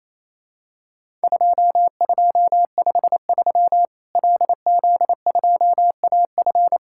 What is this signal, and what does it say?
Morse code, heard as a keyed tone, about 27 wpm, 700 Hz. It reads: 2253 LZ2AF
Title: Worked All Europe DX Contest - QTC Exchange - 28wpm